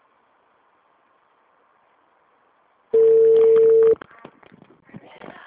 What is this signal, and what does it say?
Ringback tone (China, 450 Hz), with speech.